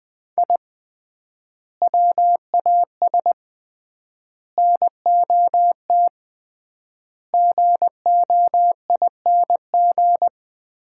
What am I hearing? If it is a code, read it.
Morse code, heard as a keyed tone, about 21 wpm, 700 Hz. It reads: I WAS NOT GOING
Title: 3x Word-Spacing - Sets of 5 Words - 20wpm